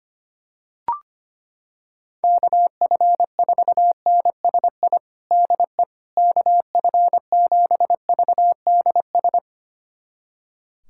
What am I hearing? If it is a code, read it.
Morse code, heard as a keyed tone, about 25 wpm, 700 Hz. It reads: KF4NSI DE KF7VDS
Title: USA Callsign DE Callsign - 6 Characters - 25wpm